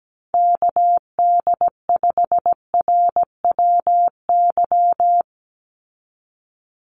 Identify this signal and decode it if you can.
Morse code, heard as a keyed tone, about 17 wpm, 700 Hz. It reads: KD5RWY
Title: USA Call Signs 17wpm